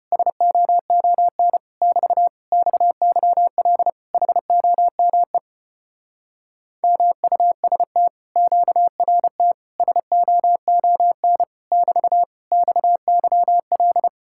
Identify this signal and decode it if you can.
Morse code, heard as a keyed tone, about 33 wpm, 700 Hz. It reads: SOON <BT> XYL HOME MUST QRT SOON <BT> XYL